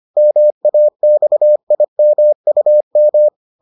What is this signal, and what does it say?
Morse code, heard as a keyed tone, about 25 wpm, 600 Hz. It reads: MAXIMUM